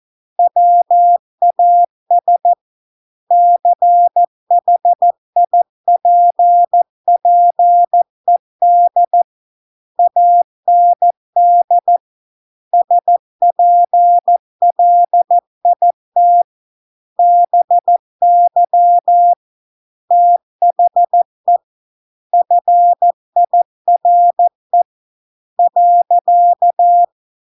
Morse code, heard as a keyed tone, about 14 wpm, 700 Hz. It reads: WAS CHIPPED AND SPLIT BY THE FIRE .